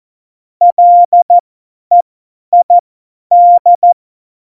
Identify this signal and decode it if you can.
Morse code, heard as a keyed tone, about 14 wpm, spaced as for 10 wpm, 700 Hz. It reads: LEID
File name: Morsekurs / Witze / Witz 06. Oktober 2020, 50 F70